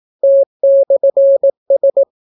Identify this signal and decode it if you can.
Morse code, heard as a keyed tone, about 18 wpm, 550 Hz. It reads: T/S